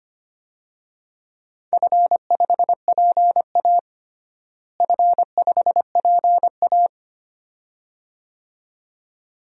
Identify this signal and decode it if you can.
Morse code, heard as a keyed tone, about 25 wpm, 700 Hz. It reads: F5PA F5PA
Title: DXCC-100 Call Signs - 4 Characters - Repeated 2x - 25wpm